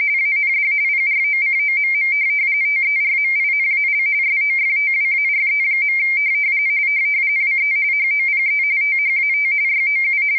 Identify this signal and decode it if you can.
RTTY, about 45.45 baud. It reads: AND PROCOMM TERMINAL SOFTWARE. WE HOPE THAT YOU